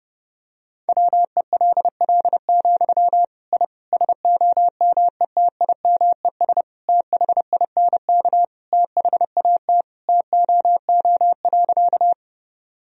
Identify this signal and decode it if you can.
Morse code, heard as a keyed tone, about 32 wpm, 700 Hz. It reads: WELL, I SOMETIMES THINK THAT TOO.
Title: Sentences from Top 1000 Words - Review Part 3 - 30wpm